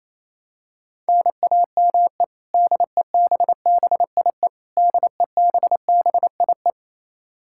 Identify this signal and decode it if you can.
Morse code, heard as a keyed tone, about 28 wpm, 700 Hz. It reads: NAME DEBBIE DEBBIE
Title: Practice QSO - Exchange 1 - Beginning - 28wpm